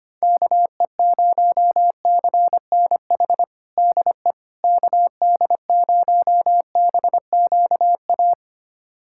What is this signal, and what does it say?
Morse code, heard as a keyed tone, about 25 wpm, 700 Hz. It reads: KE0CNH DE KD0BQA
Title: USA Callsign DE Callsign - 6 Characters - 25wpm